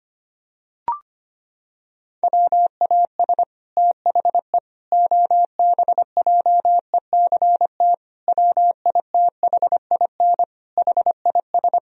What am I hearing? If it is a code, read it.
Morse code, heard as a keyed tone, about 25 wpm, 700 Hz. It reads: WAS THE OBJECT WITHIN HIS
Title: Sentences from Top 1000 Words - Review Part 4 - 25wpm